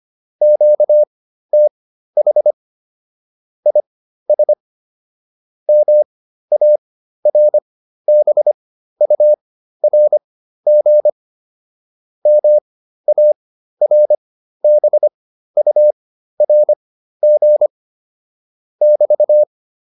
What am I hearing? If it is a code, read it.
Morse code, heard as a keyed tone, about 25 wpm, spaced as for 13 wpm, 600 Hz. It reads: QTH IS MARBURG MARBURG <BT>